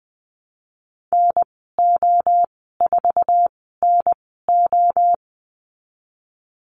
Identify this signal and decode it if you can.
Morse code, heard as a keyed tone, about 20 wpm, spaced as for 15 wpm, 700 Hz. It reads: NO4NO
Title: 055 Lesson 10g - Letter r - Call Signs 20wpm